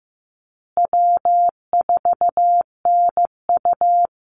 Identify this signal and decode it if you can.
Morse code, heard as a keyed tone, about 15 wpm, 700 Hz. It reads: W4NU
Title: Easy Call Signs Repeated 3x - 15wpm